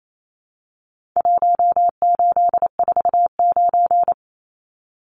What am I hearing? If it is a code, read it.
Morse code, heard as a keyed tone, about 28 wpm, 700 Hz. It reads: 1849
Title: Numbers - Contest Style - 4-Digit 28wpm